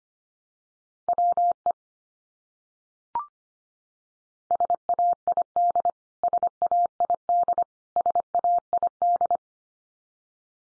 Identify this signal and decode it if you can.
Morse code, heard as a keyed tone, about 25 wpm, 700 Hz. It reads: WE SAID SAID SAID
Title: Top 100 Words 25wpm x3